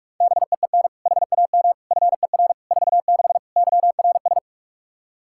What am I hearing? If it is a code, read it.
Morse code, heard as a keyed tone, about 41 wpm, 700 Hz. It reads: BEEN HAM FER 46 YRS
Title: K1USN SST QSO Practice 45wpm